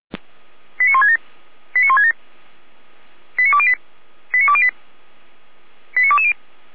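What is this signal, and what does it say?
Five-tone selective call (ZVEI), heard as quick runs of five tones, tones of about 70 ms. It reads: 89167 89167 89208 89208 89220
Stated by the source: ZVEI five-tone selective call received via SDR.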